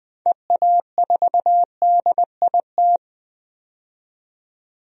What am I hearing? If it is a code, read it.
Morse code, heard as a keyed tone, about 20 wpm, 700 Hz. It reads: EA4DIT
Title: DXCC-100 Call Signs - 6 Characters - 20wpm